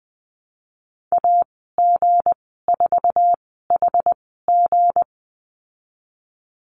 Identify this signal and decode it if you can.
Morse code, heard as a keyed tone, about 19 wpm, spaced as for 14 wpm, 700 Hz. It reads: AG4HG
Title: 179 Lesson 29g - Slash - Call Signs 20wpm